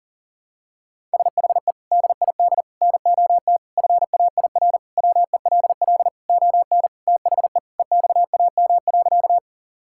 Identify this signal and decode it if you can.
Morse code, heard as a keyed tone, about 40 wpm, 700 Hz. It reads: SHE DID NOT FAIR WELL ON THE EXAM.
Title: Sentences from Top 1000 Words - Review Part 5 - 40wpm